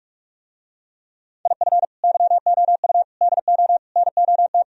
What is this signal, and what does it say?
Morse code, heard as a keyed tone, about 44 wpm, 700 Hz. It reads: IF YOU DO NOT